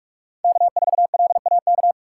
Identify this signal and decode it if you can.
Morse code, heard as a keyed tone, about 44 wpm, 700 Hz. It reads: K4LAK
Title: USA Call Signs - 5 Characters - 6x Character Spacing - 45wpm